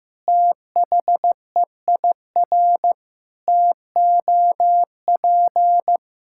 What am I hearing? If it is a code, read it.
Morse code, heard as a keyed tone, about 15 wpm, 700 Hz. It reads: THEIR TOP
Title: Sentences from Top 1000 Words - Review Part 5 - 15wpm